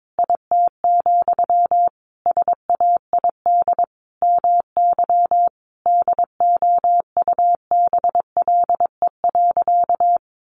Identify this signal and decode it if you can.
Morse code, heard as a keyed tone, about 22 wpm, 700 Hz. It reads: IT, SAID MY DOUBLE.
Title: Sentences from Top 1400 Words 22wpm